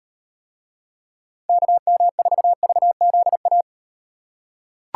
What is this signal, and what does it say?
Morse code, heard as a keyed tone, about 38 wpm, 700 Hz. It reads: KM4VZA